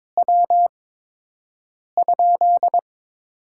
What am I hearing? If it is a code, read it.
Morse code, heard as a keyed tone, about 22 wpm, 700 Hz. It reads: W ?
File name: Top 20 QSO Elements - 8x Character Spacing - 22wpm